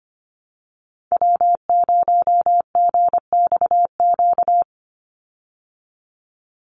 Morse code, heard as a keyed tone, about 25 wpm, 700 Hz. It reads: W0GXQ
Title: Call Signs Repeated 3x - 25wpm